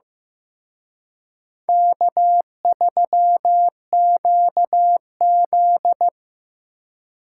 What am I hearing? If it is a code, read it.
Morse code, heard as a keyed tone, about 18 wpm, 700 Hz. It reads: K3QZ
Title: Speed Racing - USA 1x2 Call Signs 15wpm